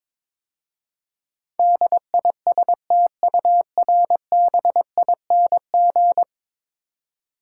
Morse code, heard as a keyed tone, about 22 wpm, 700 Hz. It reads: DISTURBING
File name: practice / 22wpm / 10 Letter Words - First 3 Letters Given - 22wpm